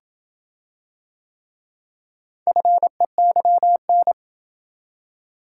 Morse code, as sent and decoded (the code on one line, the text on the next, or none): ..-. -.-- -.
FEYN